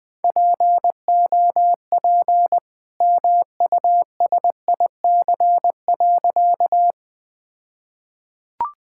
Morse code, heard as a keyed tone, about 20 wpm, 700 Hz. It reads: POP MUSIC.
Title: All-Age Appropriate Jokes - 20wpm